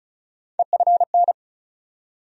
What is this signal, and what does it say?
Morse code, heard as a keyed tone, about 36 wpm, 700 Hz. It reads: EFN
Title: USA Call Sign Postfixes - 3 Letters 35wpm